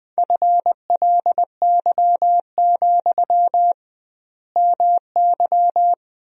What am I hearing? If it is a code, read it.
Morse code, heard as a keyed tone, about 21 wpm, 700 Hz. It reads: FLY, MY
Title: 2x Word-Spacing - Sentences from Top 500 Words - 20wpm